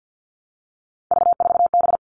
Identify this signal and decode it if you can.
Morse code, heard as a keyed tone, about 50 wpm, 700 Hz. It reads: V4B
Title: All Cut Numbers - 3-Digit 50wpm